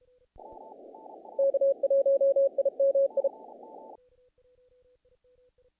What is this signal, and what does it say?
Morse code, heard as a keyed tone, about 35 wpm, 550 Hz. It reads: K1IMI
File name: contests / naqp-cw-k5tr-op-k5pi- -1800